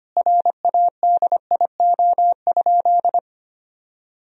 Morse code, heard as a keyed tone, about 25 wpm, 700 Hz. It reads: RADIO?
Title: Sentences from Top 1000 Words - Review Part 3 - 25wpm